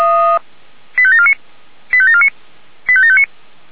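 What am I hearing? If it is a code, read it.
Five-tone selective call (ZVEI), heard as quick runs of five tones, tones of about 70 ms. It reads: 85749 85749 85759
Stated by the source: ZVEI five-tone selective call received by SDR.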